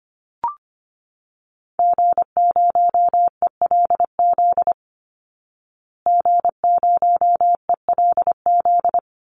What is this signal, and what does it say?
Morse code, heard as a keyed tone, about 25 wpm, 700 Hz. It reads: G0ELZ G0ELZ
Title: Call Signs Repeated 3x - 25wpm